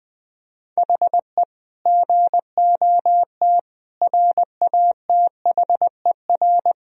Morse code, heard as a keyed tone, about 20 wpm, 700 Hz. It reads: HE GOT RATHER
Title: Sentences from Top 1000 Words - Review Part 4 - 20wpm